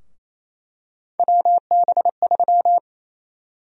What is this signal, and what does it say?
Morse code, heard as a keyed tone, about 27 wpm, 700 Hz. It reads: WB3
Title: USA Call Sign Prefixes - Repeated 2x - 28wpm